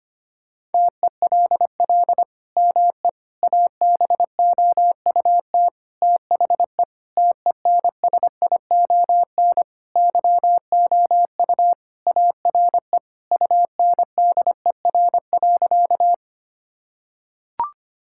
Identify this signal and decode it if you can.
Morse code, heard as a keyed tone, about 25 wpm, 700 Hz. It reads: TELL ME ABOUT THE TENSION YOU ARE UNDER.
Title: Sentences from Top 1600 Words 25wpm